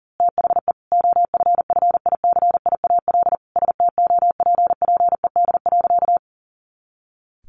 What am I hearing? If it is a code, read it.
Morse code, heard as a keyed tone, about 37 wpm, 700 Hz. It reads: THE OFFICIAL STOPPED.
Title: A Sound Of Thunder 40wpm